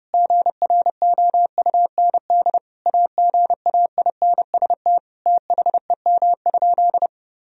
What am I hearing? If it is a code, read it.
Morse code, heard as a keyed tone, about 32 wpm, 700 Hz. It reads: GROUND AGAINST THEM?
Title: Sentences from Top 1000 Words - Review Part 2 - 30wpm